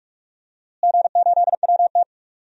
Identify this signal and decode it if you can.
Morse code, heard as a keyed tone, about 42 wpm, 700 Hz. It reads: G8WT
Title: DXCC-100 Call Signs - 4 Characters - Repeated 2x - 45wpm